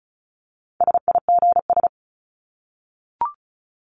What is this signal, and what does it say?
Morse code, heard as a keyed tone, about 34 wpm, 700 Hz. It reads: SIGS
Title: Abbreviations 35wpm